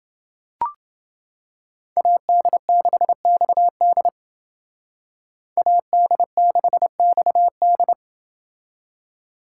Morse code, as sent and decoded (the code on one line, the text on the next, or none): .- -.. -.... -..- -.. / .- -.. -.... -..- -..
AD6XD AD6XD